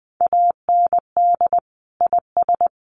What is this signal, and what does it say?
Morse code, heard as a keyed tone, about 20 wpm, 700 Hz. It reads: AND IS
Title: Sentences from Top 1300 Words 20wpm